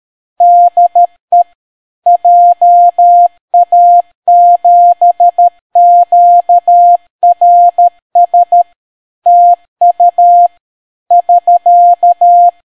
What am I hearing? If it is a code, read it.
Morse code, heard as a keyed tone, about 13 wpm, 700 Hz. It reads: DE JA7QRS TU <SK>